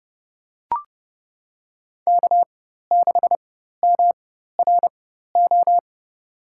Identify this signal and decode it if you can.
Morse code, heard as a keyed tone, about 30 wpm, spaced as for 14 wpm, 700 Hz. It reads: K6MRO
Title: USA Call Signs - 5 Characters - 4x Character Spacing - 30wpm